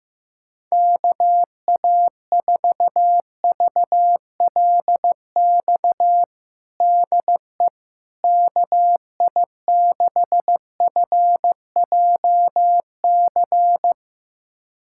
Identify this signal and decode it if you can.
Morse code, heard as a keyed tone, about 15 wpm, 700 Hz. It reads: KA4VLX DE KI6FJC
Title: USA Callsign DE Callsign - 6 Characters - 15wpm